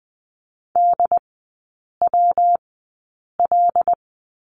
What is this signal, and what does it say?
Morse code, heard as a keyed tone, about 20 wpm, spaced as for 8 wpm, 700 Hz. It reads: DWL
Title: 184 Lesson 30e - Letter b - Three Characters 20wpm